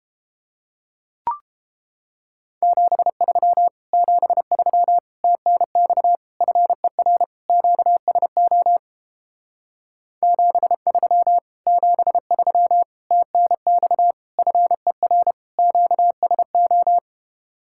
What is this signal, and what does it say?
Morse code, heard as a keyed tone, about 31 wpm, 700 Hz. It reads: 73 73 TNX FER QSO 73 73 TNX FER QSO